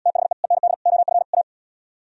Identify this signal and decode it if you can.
Morse code, heard as a keyed tone, about 76 wpm, 700 Hz. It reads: THE END OF A